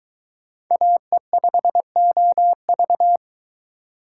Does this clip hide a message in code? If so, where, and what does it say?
Yes: Morse code from 0.7 to 3.2 s, heard as a keyed tone, about 22 wpm, 700 Hz. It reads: AE5OV